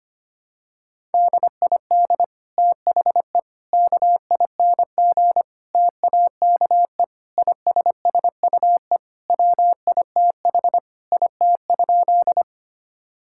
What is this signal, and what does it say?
Morse code, heard as a keyed tone, about 25 wpm, 700 Hz. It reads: DID THE KING TAKE ISSUE WITH IT?
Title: Sentences from Top 1000 Words - Review Part 6 - 25wpm